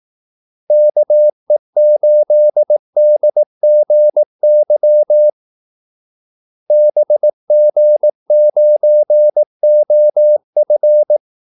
Morse code, heard as a keyed tone, about 18 wpm, 600 Hz. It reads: KE8DGY BG9OF
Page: Kaldesignaler 18 wpm